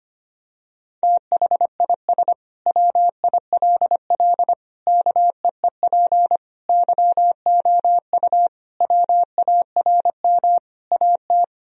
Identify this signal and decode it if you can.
Morse code, heard as a keyed tone, about 25 wpm, 700 Hz. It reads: THIS WILL KEEP YOU WARM AT
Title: Sentences from Top 1000 Words - Review Part 2 - 25wpm